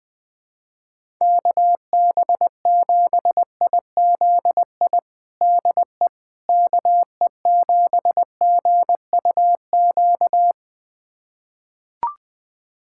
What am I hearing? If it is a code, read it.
Morse code, heard as a keyed tone, about 20 wpm, 700 Hz. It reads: KB7IZI DE KE7GUQ
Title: USA Callsign DE Callsign - 6 Characters - 20wpm